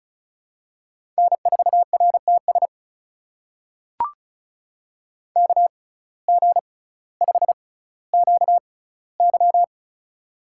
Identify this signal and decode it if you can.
Morse code, heard as a keyed tone, about 36 wpm, spaced as for 18 wpm, 700 Hz. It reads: N4RTS KG5QY
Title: USA Call Signs - 5 Characters - 6x Character Spacing - 35wpm